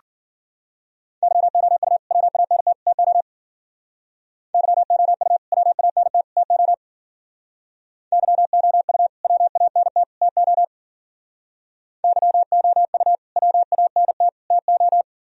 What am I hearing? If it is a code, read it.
Morse code, heard as a keyed tone, about 50 wpm, 700 Hz. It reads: YOU WANT TO YOU WANT TO YOU WANT TO YOU WANT TO
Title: Speed-Racing - Sets of 3 Words 40wpm